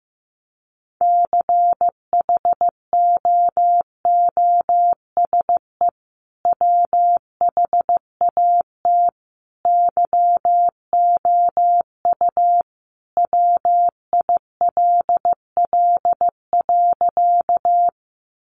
Morse code, heard as a keyed tone, about 15 wpm, 700 Hz. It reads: CHOOSE WHAT YOU WILL.
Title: Sentences from Top 700 Words 15wpm